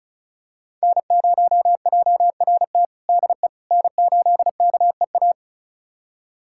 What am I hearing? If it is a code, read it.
Morse code, heard as a keyed tone, about 34 wpm, 700 Hz. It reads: N0JRT DE N8KEA
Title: USA Callsign DE Callsign - 5 Characters - 35wpm